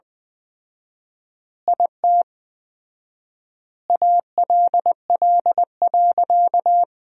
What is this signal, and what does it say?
Morse code, heard as a keyed tone, about 20 wpm, 700 Hz. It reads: IT ALL.
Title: 4x Word-Spacing - Sentences from Top 200 Words - 20wpm